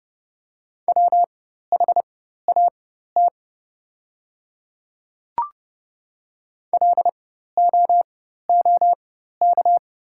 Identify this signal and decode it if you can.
Morse code, heard as a keyed tone, about 31 wpm, spaced as for 14 wpm, 690 Hz. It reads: WHAT LOOK
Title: 4x Character Spacing - Top 200 Words - 30wpm